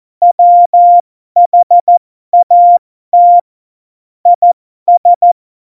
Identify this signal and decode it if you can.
Morse code, heard as a keyed tone, about 14 wpm, spaced as for 12 wpm, 700 Hz. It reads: WHAT IS
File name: Morsekurs / Witze